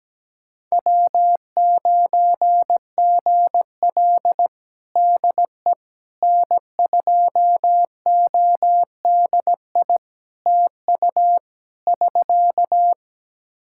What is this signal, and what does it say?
Morse code, heard as a keyed tone, about 17 wpm, 700 Hz. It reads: W9GL DE N2ODI TU <SK>
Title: Practice QSO - Exchange 4 - Ending - 17wpm